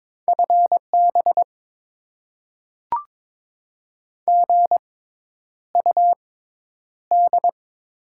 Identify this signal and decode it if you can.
Morse code, heard as a keyed tone, about 21 wpm, spaced as for 10 wpm, 700 Hz. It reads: FB GUD
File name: Top 20 QSO Elements - 6x Character Spacing - 22wpm